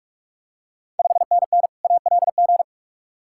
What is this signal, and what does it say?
Morse code, heard as a keyed tone, about 46 wpm, 690 Hz. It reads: 5NN ALG
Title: Ontario QSO Party Exchange 45wpm